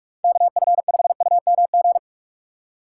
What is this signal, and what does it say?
Morse code, heard as a keyed tone, about 46 wpm, 700 Hz. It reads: KF5UMG
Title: USA Call Signs - 6 Characters - 2x Character Spacing - 45wpm